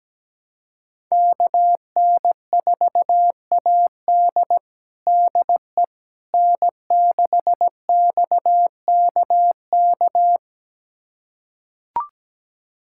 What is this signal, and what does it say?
Morse code, heard as a keyed tone, about 17 wpm, 700 Hz. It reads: KN4AD DE N6XKK
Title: USA Callsign DE Callsign - 5 Characters - 17wpm